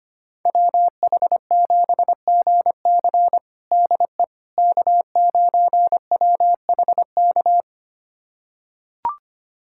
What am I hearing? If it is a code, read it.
Morse code, heard as a keyed tone, about 25 wpm, 700 Hz. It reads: WH7GC DE K9WHK
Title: USA Callsign DE Callsign - 5 Characters - 25wpm